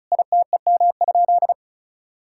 Morse code, heard as a keyed tone, about 36 wpm, 700 Hz. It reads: ITEM?
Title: Sentences from Top 1000 Words - Review Part 6 - 35wpm